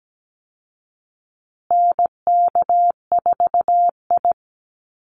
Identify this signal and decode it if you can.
Morse code, heard as a keyed tone, about 17 wpm, 700 Hz. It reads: NK4I
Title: Call Signs Repeated 2x - 17wpm